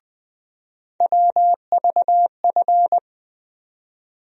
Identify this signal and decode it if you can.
Morse code, heard as a keyed tone, about 21 wpm, 700 Hz. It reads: WVF
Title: USA Call Sign Postfixes - 3 Letters 20wpm